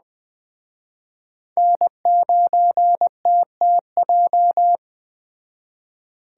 Morse code, heard as a keyed tone, about 20 wpm, 700 Hz. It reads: N9TTJ
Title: Speed Racing - USA Call Signs 17wpm